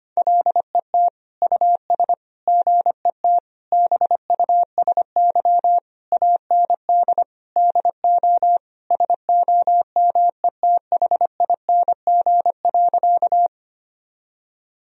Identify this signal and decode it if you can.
Morse code, heard as a keyed tone, about 25 wpm, 700 Hz. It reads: LET US GET BUSY AND DO SOMETHING.
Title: Sentences from Top 1000 Words - Review Part 3 - 25wpm